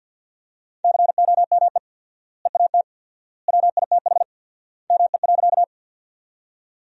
Morse code, heard as a keyed tone, about 46 wpm, 700 Hz. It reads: COME EAT WITH ME.
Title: 4x Word-Spacing - Sentences from Top 300 Words - 50wpm